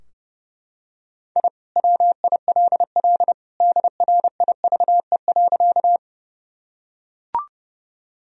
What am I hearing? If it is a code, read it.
Morse code, heard as a keyed tone, about 32 wpm, 700 Hz. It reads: I WILL DRIVE.
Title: Sentences from Top 1000 Words - Review Part 2 - 30wpm